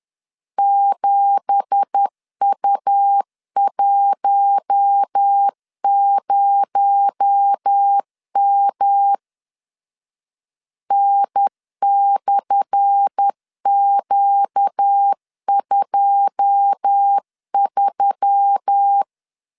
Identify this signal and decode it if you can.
Morse code, heard as a keyed tone, about 10 wpm, 800 Hz. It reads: 7U10M N/Q23
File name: LEZIONI / 10 SIGNS / 10A SLASH 10wpm 2808